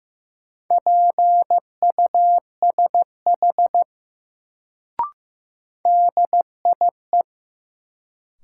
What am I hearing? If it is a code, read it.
Morse code, heard as a keyed tone, about 15 wpm, 700 Hz. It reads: PUSH DIE
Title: Sentences from Top 1300 Words - New Word Drill Alt - 15wpm